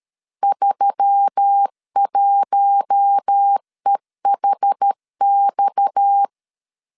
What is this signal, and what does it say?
Morse code, heard as a keyed tone, about 13 wpm, 800 Hz. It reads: 31EHX
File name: CW / DET / 1 9 1500C S A 12wpm 3800